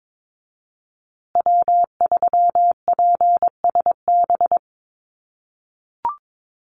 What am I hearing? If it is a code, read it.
Morse code, heard as a keyed tone, about 22 wpm, 700 Hz. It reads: W3PSB
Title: USA Call Signs Repeated 2x 22wpm